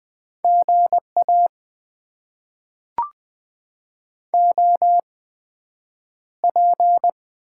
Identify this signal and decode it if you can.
Morse code, heard as a keyed tone, about 20 wpm, spaced as for 9 wpm, 700 Hz. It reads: GA OP
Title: Top 90 QSO Elements - 8x Character Spacing - 20wpm